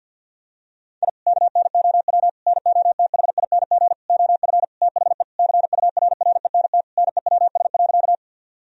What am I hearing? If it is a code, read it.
Morse code, heard as a keyed tone, about 50 wpm, 700 Hz. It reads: I KNOW NOTHING OF THE CURRENT NEWS.
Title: Sentences from Top 1000 Words - Review Part 4 - 50wpm